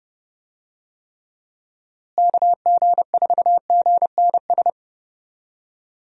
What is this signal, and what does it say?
Morse code, heard as a keyed tone, about 30 wpm, 700 Hz. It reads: KG4GNS
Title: USA Call Signs - 6 Characters - Repeated 2x - 30wpm